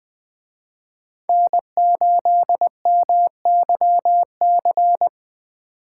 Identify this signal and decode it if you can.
Morse code, heard as a keyed tone, about 21 wpm, 700 Hz. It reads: N8MYC
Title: Speed Racing - USA Call Signs 17wpm